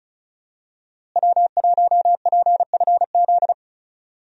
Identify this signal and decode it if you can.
Morse code, heard as a keyed tone, about 36 wpm, 700 Hz. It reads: W1PFZ
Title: USA Call Signs - 5 Characters - Repeated 3x - 35wpm